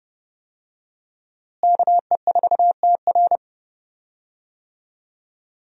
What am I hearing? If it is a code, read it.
Morse code, heard as a keyed tone, about 30 wpm, 700 Hz. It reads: KE4TR
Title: USA Call Signs - 5 Characters - Repeated 3x - 30wpm